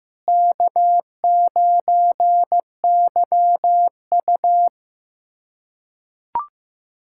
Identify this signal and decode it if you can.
Morse code, heard as a keyed tone, about 15 wpm, 700 Hz. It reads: K9YU
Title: USA Callsign DE Callsign - 4 Characters - 15wpm